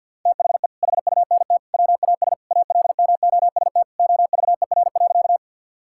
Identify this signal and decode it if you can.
Morse code, heard as a keyed tone, about 52 wpm, 700 Hz. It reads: THE HUNT WAS ALMOST OVER.
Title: Sentences from Top 1600 Words 50wpm